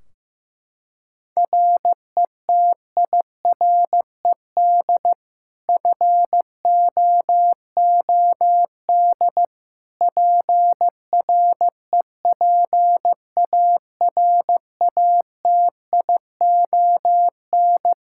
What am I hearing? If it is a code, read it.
Morse code, heard as a keyed tone, about 15 wpm, 700 Hz. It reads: RETIRED FOOD PREPARATION
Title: K1USN SST QSO Practice 15wpm